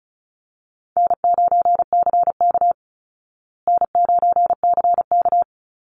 Morse code, heard as a keyed tone, about 34 wpm, 700 Hz. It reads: N9CK N9CK
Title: Easy Call Signs Repeated 2x - 35wpm